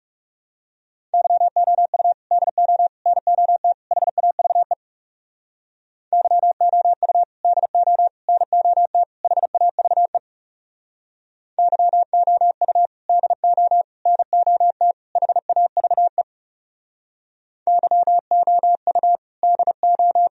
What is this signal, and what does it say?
Morse code, heard as a keyed tone, about 40 wpm, 700 Hz. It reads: YOU DO NOT HAVE YOU DO NOT HAVE YOU DO NOT HAVE YOU DO